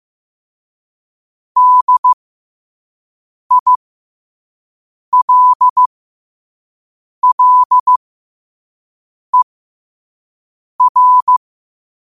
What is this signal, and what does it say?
Morse code, heard as a keyed tone, about 15 wpm, spaced as for 5 wpm, 1000 Hz. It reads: DILLER
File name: qotdfiles / 5wpm